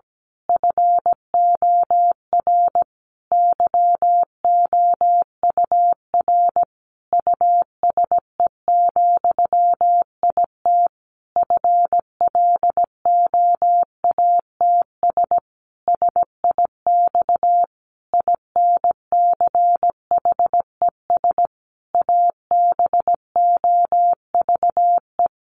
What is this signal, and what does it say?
Morse code, heard as a keyed tone, about 17 wpm, 700 Hz. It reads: FOR YOUR USE, IT FLOATS SIX INCHES ABOVE
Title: A Sound Of Thunder 17wpm